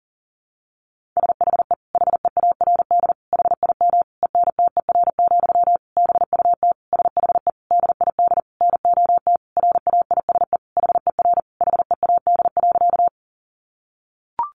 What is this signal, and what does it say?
Morse code, heard as a keyed tone, about 40 wpm, 700 Hz. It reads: SHE HEARD HIM ENTER, BUT SHE DID NOT RAISE HER HEAD.
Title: Sentences from Top 1400 Words 40wpm